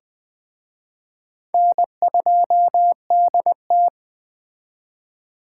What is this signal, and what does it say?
Morse code, heard as a keyed tone, about 20 wpm, 700 Hz. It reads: N2DT
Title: Speed Racing - USA Call Signs 17wpm